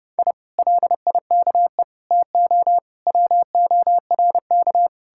Morse code, heard as a keyed tone, about 40 wpm, spaced as for 35 wpm, 700 Hz. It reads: I LIKE TO WORK
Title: Speed-Racing Common QSO Phrases Encore - 30wpm